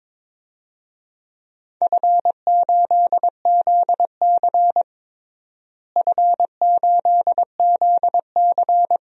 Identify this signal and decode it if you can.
Morse code, heard as a keyed tone, about 22 wpm, 700 Hz. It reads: F8ZC F8ZC
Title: DXCC-100 Call Signs - 4 Characters - Repeated 3x - 22wpm